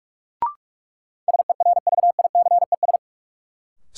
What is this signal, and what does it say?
Morse code, heard as a keyed tone, about 46 wpm, 700 Hz. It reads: SERVICES
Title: Sets of 5 Words - New Word Drill - 45wpm